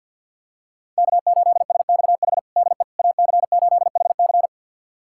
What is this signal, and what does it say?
Morse code, heard as a keyed tone, about 49 wpm, 700 Hz. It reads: K8SXH DE AC8HC
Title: USA Callsign DE Callsign 50wpm